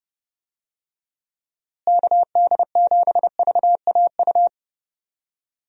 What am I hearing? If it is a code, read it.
Morse code, heard as a keyed tone, about 30 wpm, 700 Hz. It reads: KD7VAU
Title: USA Call Signs - 6 Characters - Repeated 2x - 30wpm